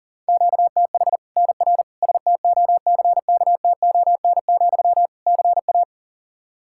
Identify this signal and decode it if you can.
Morse code, heard as a keyed tone, about 37 wpm, 700 Hz. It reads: QTH NR STOCKTON, CA
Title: Practice QSO - Exchange 1 - Beginning - 40wpm